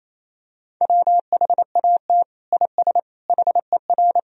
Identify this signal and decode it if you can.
Morse code, heard as a keyed tone, about 28 wpm, 700 Hz. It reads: WHAT IS HER
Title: Sentences from Top 1000 Words - Review Part 1 - 28wpm